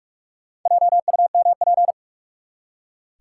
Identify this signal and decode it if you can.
Morse code, heard as a keyed tone, about 44 wpm, 700 Hz. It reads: JUMP
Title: Four Letter Words - 45wpm